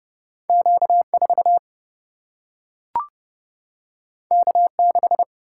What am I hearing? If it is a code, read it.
Morse code, heard as a keyed tone, about 30 wpm, 700 Hz. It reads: Q 4 K 6
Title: Single-Letter Single-Number 30wpm